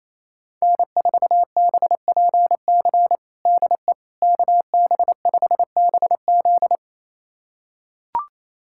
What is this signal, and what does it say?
Morse code, heard as a keyed tone, about 28 wpm, 700 Hz. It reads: N4BPC DE KB5BZ
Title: USA Callsign DE Callsign - 5 Characters - 28wpm